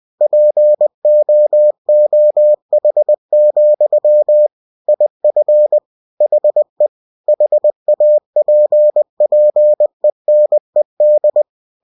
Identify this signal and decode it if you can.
Morse code, heard as a keyed tone, about 20 wpm, 600 Hz. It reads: POOH, IF HE HAPPENED
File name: cw-books / winnie-the-pooh / winnie-20wpm-page-0036